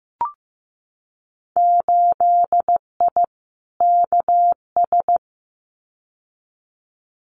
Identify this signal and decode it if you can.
Morse code, heard as a keyed tone, about 15 wpm, 700 Hz. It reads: 8I KS
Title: Winter Field Day Exchanges Encore 15wpm